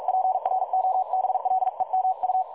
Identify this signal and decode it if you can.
Morse code, heard as a keyed tone, about 45 wpm, 750 Hz. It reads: PROBLEMU
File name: dlazaawansowanych / nagraniazpasma / websdr recording 3565.0kHz